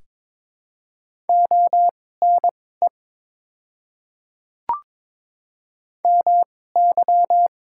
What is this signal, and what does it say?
Morse code, heard as a keyed tone, about 22 wpm, spaced as for 16 wpm, 700 Hz. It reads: ONE MY
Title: 2x Character Spacing - Top 200 Words - 22wpm